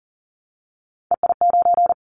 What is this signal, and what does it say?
Morse code, heard as a keyed tone, about 41 wpm, 700 Hz. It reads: EI9
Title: Callsign Prefixes for Top 50 DXCC - 40wpm